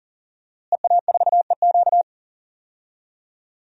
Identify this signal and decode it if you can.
Morse code, heard as a keyed tone, about 40 wpm, 700 Hz. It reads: EA4EQ
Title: DXCC-100 Call Signs - 5 Characters - Repeated 3x - 40wpm